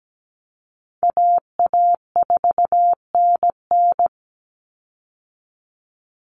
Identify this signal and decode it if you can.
Morse code, heard as a keyed tone, about 17 wpm, 700 Hz. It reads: AA4NN
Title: Call Signs Repeated 3x - 17wpm